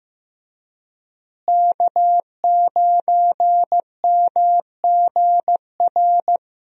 Morse code, heard as a keyed tone, about 15 wpm, 700 Hz. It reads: K9MGR
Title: USA Call Signs - 5 Characters - Repeated 3x - 15wpm